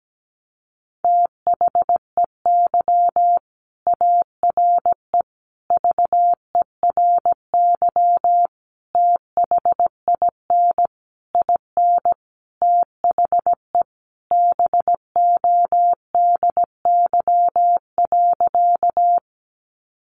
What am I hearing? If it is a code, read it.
Morse code, heard as a keyed tone, about 17 wpm, 700 Hz. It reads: THEY ARE VERY THIN IN THE BODY.
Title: Sentences from Top 1200 Words 17wpm